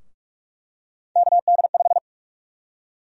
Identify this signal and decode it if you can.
Morse code, heard as a keyed tone, about 42 wpm, 700 Hz. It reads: KD5
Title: USA Call Sign Prefixes - 45wpm